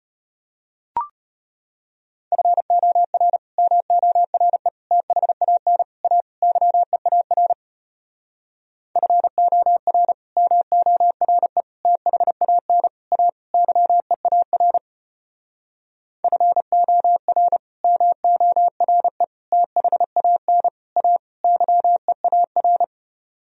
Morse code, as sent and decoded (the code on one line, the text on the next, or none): ..-. --- .-. / -- --- .-. / - .... .- -. / .- / -.-- . .- .-. / ..-. --- .-. / -- --- .-. / - .... .- -. / .- / -.-- . .- .-. / ..-. --- .-. / -- --- .-. / - .... .- -. / .- / -.-- . .- .-.
FOR MORE THAN A YEAR FOR MORE THAN A YEAR FOR MORE THAN A YEAR